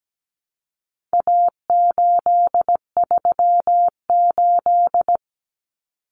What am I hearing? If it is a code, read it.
Morse code, heard as a keyed tone, about 17 wpm, 700 Hz. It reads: A838
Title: Common Cut Numbers - 4-Digit 17wpm